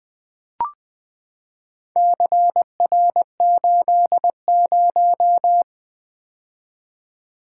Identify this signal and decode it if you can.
Morse code, heard as a keyed tone, about 20 wpm, 700 Hz. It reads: CR80
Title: Four Letter-Number Combinations - 20wpm